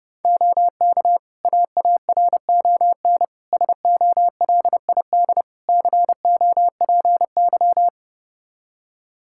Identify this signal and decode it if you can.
Morse code, heard as a keyed tone, about 30 wpm, 700 Hz. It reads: OK AARON SOLID COPY
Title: K1USN SST QSO Practice 30wpm